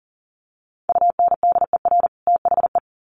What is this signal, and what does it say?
Morse code, heard as a keyed tone, about 40 wpm, 700 Hz. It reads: UNDER THE
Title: Sets of 2 Words 40wpm